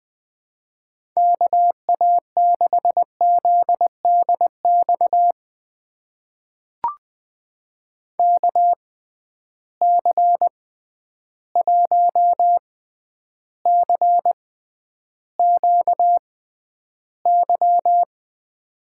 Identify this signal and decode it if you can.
Morse code, heard as a keyed tone, about 21 wpm, spaced as for 10 wpm, 700 Hz. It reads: KA6ZDX KC1CQY
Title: USA Call Signs - 6 Characters - 6x Character Spacing - 20wpm